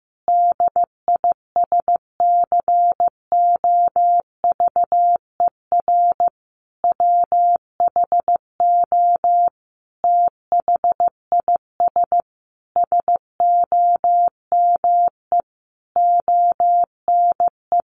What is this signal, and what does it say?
Morse code, heard as a keyed tone, about 15 wpm, 700 Hz. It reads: DISCOVER WHO THIS SOME ONE
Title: Sentences from Top 1100 Words 15wpm